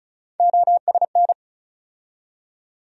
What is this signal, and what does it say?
Morse code, heard as a keyed tone, about 34 wpm, 700 Hz. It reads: OSN
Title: USA Call Sign Postfixes - 3 Letters 35wpm